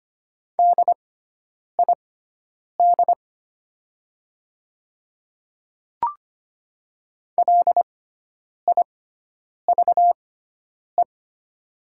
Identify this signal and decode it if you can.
Morse code, heard as a keyed tone, about 25 wpm, spaced as for 9 wpm, 700 Hz. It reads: DID LIVE
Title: 6x Character Spacing - Top 300 Words - 25wpm